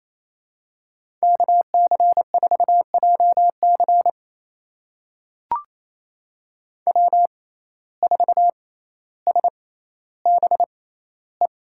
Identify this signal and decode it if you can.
Morse code, heard as a keyed tone, about 28 wpm, spaced as for 14 wpm, 700 Hz. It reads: KC4JC W4SBE
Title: USA Call Signs - 5 Characters - 6x Character Spacing - 28wpm